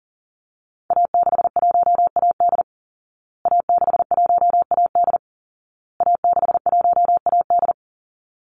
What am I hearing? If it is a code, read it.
Morse code, heard as a keyed tone, about 40 wpm, 700 Hz. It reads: A61AD A61AD A61AD
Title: Call Signs Repeated 2x - 40wpm